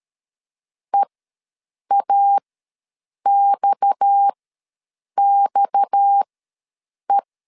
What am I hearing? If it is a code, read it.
Morse code, heard as a keyed tone, about 13 wpm, spaced as for 7 wpm, 800 Hz. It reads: EAXXE
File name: LEZIONI / 6 PKX0 / 6 PKX ws 12wpm 3106